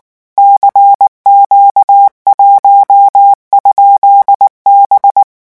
Morse code, heard as a keyed tone, about 19 wpm, 800 Hz. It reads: CQ1?B
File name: morse mp3 / revision15-vitesse 18 mots